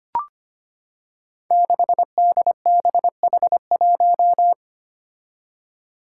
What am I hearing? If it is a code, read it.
Morse code, heard as a keyed tone, about 25 wpm, 700 Hz. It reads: 6DBH1
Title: Five Challenging Letter-Number Combinations 25wpm